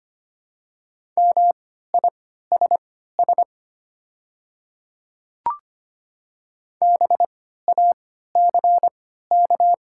Morse code, heard as a keyed tone, about 25 wpm, spaced as for 14 wpm, 700 Hz. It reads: MISS BACK
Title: 3x Character Spacing - Top 500 Words - 25wpm